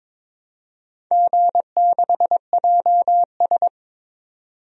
Morse code, heard as a keyed tone, about 22 wpm, 700 Hz. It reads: G6JS
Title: DXCC-100 Call Signs - 4 Characters - 22wpm